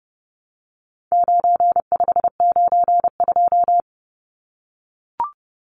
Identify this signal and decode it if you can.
Morse code, heard as a keyed tone, about 30 wpm, 700 Hz. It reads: 9592
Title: Numbers - 4-Digit 30wpm